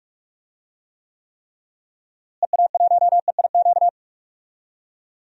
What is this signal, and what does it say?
Morse code, heard as a keyed tone, about 43 wpm, 700 Hz. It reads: EA1EIQ
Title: DXCC-100 Call Signs - 6 Characters - Repeated 3x - 45wpm